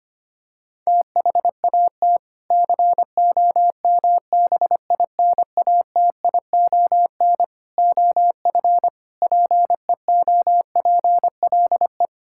Morse code, heard as a keyed tone, about 25 wpm, 700 Hz. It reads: THAT COMBINATION OF PEOPLE
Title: Sentences from Top 1000 Words - Review Part 6 - 25wpm